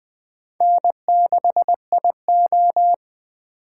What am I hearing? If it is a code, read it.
Morse code, heard as a keyed tone, about 21 wpm, 700 Hz. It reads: N6IO
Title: Speed Racing - USA Call Signs 17wpm